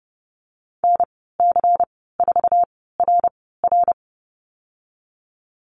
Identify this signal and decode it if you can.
Morse code, heard as a keyed tone, about 30 wpm, spaced as for 17 wpm, 700 Hz. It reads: NC4RR
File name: Lesson 19g - Letter w - Call Signs 30wpm